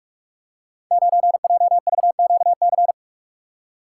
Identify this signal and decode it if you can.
Morse code, heard as a keyed tone, about 47 wpm, 700 Hz. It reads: 9JVQC
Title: Code Groups - 5 Letter:Numbers - 45wpm